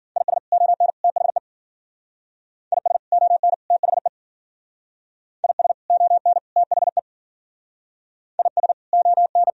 Morse code, heard as a keyed tone, about 54 wpm, 700 Hz. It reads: IS ON THE IS ON THE IS ON THE IS ON